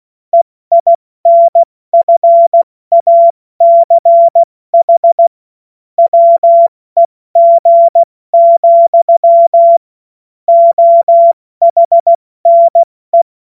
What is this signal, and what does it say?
Morse code, heard as a keyed tone, about 16 wpm, spaced as for 14 wpm, 680 Hz. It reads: EINFACH WEG, OHNE
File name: Morsekurs / Witze / Witz 01. November 2022 70 F80